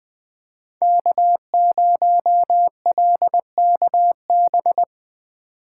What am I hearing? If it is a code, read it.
Morse code, heard as a keyed tone, about 21 wpm, 700 Hz. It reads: K0LKB
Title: USA Call Signs - 5 Characters - 3x Character Spacing - 20wpm